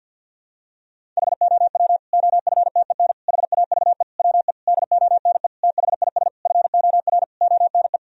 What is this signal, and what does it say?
Morse code, heard as a keyed tone, about 48 wpm, 700 Hz. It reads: HOW OFTEN HAVE WE DONE THIS FOR ONE
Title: Sentences from Top 1000 Words - Review Part 1 - 50wpm